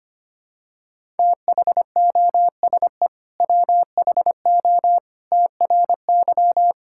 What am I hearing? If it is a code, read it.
Morse code, heard as a keyed tone, about 25 wpm, 700 Hz. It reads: THOSE WHO TRY
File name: practice / 25wpm / Sentences with 2-Word Phrases - 25wpm